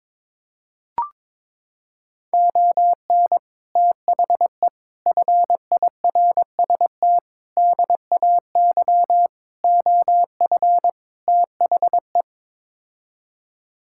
Morse code, heard as a keyed tone, about 22 wpm, 700 Hz. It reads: ON THE FIRST DAY OF THE
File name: practice / 22wpm / Sets of 6 Words - 22wpm